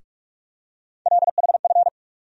Morse code, heard as a keyed tone, about 48 wpm, 700 Hz. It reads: LHF
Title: USA Call Sign Postfixes - 3 Letters 45wpm